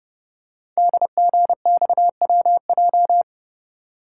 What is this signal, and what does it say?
Morse code, heard as a keyed tone, about 30 wpm, 700 Hz. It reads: DGXWJ
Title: Code Groups - 5 Letters - 30wpm